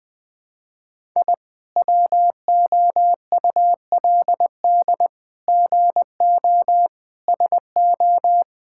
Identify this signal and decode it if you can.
Morse code, heard as a keyed tone, about 21 wpm, 700 Hz. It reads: I WOULD GO SO
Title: Sentences with 9-Word Phrases - Spoken Beginning - 20wpm